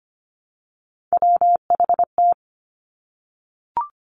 Morse code, heard as a keyed tone, about 25 wpm, 700 Hz. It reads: WHT
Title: Three Challenging Letter-Number Combinations 25wpm